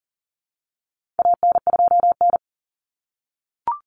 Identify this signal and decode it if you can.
Morse code, heard as a keyed tone, about 40 wpm, 700 Hz. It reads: AN2N
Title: Common Cut Numbers - 4-Digit 40wpm